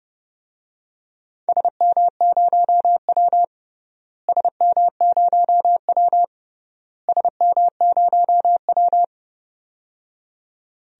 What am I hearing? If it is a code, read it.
Morse code, heard as a keyed tone, about 30 wpm, 700 Hz. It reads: SM0W SM0W SM0W
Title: DXCC-100 Call Signs - 4 Characters - Repeated 3x - 30wpm